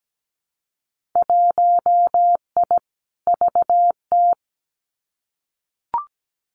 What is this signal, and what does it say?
Morse code, heard as a keyed tone, about 17 wpm, 700 Hz. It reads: 1I VT
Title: Winter Field Day Exchanges Encore 17wpm